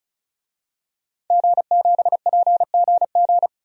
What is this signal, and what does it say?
Morse code, heard as a keyed tone, about 34 wpm, 700 Hz. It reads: G7PGG
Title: DXCC-100 Call Signs - 5 Characters - 35wpm